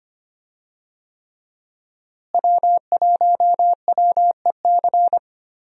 Morse code, heard as a keyed tone, about 25 wpm, 700 Hz. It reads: W1WEC
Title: USA Call Signs - 5 Characters - Repeated 3x - 25wpm